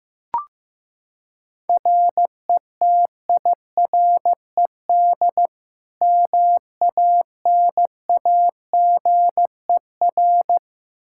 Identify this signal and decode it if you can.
Morse code, heard as a keyed tone, about 15 wpm, 700 Hz. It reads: RETIRED MANAGER
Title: K1USN SST QSO Practice 15wpm